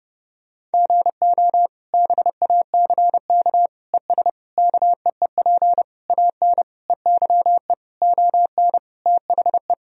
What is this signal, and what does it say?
Morse code, heard as a keyed tone, about 30 wpm, 700 Hz. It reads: GO BACK ES KEEP AN EYE ON THE
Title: Sentences from Top 300 Words and Abbreviations 30wpm